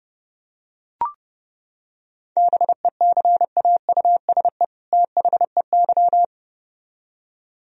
Morse code, heard as a keyed tone, about 32 wpm, 700 Hz. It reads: BECAUSE THEY